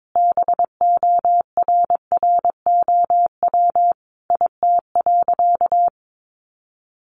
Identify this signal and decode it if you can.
Morse code, heard as a keyed tone, about 22 wpm, 700 Hz. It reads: BORROW IT.
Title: Sentences from Top 1500 Words 22wpm